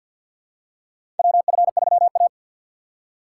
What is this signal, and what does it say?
Morse code, heard as a keyed tone, about 52 wpm, 700 Hz. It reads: WF3A
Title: USA Call Signs - 4x Character Spacing - 50wpm